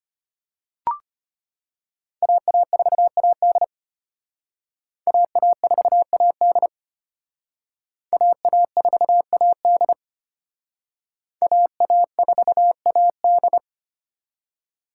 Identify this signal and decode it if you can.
Morse code, heard as a keyed tone, about 31 wpm, 700 Hz. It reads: AA4AD AA4AD AA4AD AA4AD
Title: Speed Racing - USA 2x2 Call Signs 25wpm